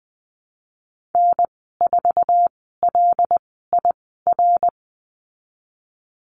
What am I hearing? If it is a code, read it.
Morse code, heard as a keyed tone, about 20 wpm, spaced as for 15 wpm, 700 Hz. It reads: N4LIR
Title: 088 Lesson 15g - Number 5 - Call Signs 20wpm